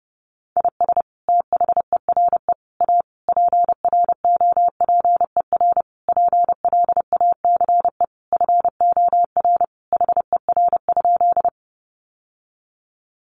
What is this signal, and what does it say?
Morse code, heard as a keyed tone, about 30 wpm, 700 Hz. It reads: IS THERE A PROPER PLACE FOR HER?
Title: Sentences from Top 800 Words 30wpm